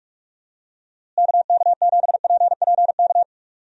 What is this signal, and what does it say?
Morse code, heard as a keyed tone, about 46 wpm, 700 Hz. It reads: KK7PPK
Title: USA Call Signs - 6 Characters - Repeated 2x - 45wpm